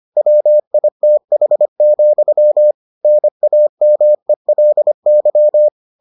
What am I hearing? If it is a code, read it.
Morse code, heard as a keyed tone, about 25 wpm, 600 Hz. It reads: WITH, NAMELY